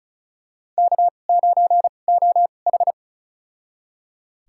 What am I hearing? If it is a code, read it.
Morse code, heard as a keyed tone, about 34 wpm, spaced as for 25 wpm, 700 Hz. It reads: K9OH
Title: USA Call Signs - 2x Character Spacing - 35wpm